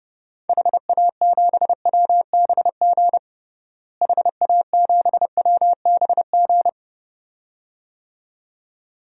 Morse code, heard as a keyed tone, about 30 wpm, 700 Hz. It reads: HA7WBG HA7WBG
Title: DXCC-100 Call Signs - 6 Characters - Repeated 2x - 30wpm